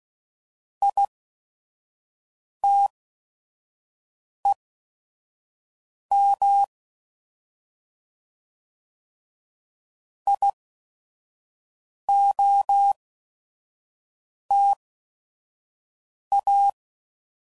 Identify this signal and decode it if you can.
Morse code, heard as a keyed tone, about 16 wpm, spaced as for 5 wpm, 800 Hz. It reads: ITEM IOTA
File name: Prefixes / By First Letter / Is